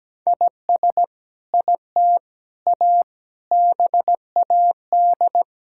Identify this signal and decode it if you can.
Morse code, heard as a keyed tone, about 17 wpm, 700 Hz. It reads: IS IT A BAD
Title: Sentences from Top 1000 Words - Review Part 5 - 17wpm